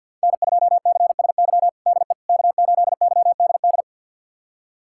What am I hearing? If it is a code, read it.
Morse code, heard as a keyed tone, about 51 wpm, 700 Hz. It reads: N2CSY DE K8YDD
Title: USA Callsign DE Callsign - 5 Characters - 50wpm